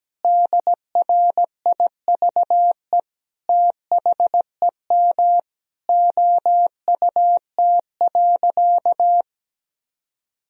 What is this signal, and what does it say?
Morse code, heard as a keyed tone, about 17 wpm, 700 Hz. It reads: DRIVE THEM OUT.
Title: Sentences from Top 500 Words and Abbreviations 17wpm